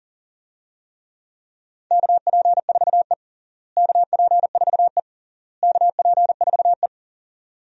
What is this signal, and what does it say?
Morse code, heard as a keyed tone, about 40 wpm, 700 Hz. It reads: KP4E KP4E KP4E
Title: DXCC-100 Call Signs - 4 Characters - Repeated 3x - 40wpm